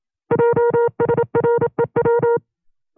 Morse code, heard as a keyed tone, about 27 wpm, 470 Hz. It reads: JSREW
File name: 25WPM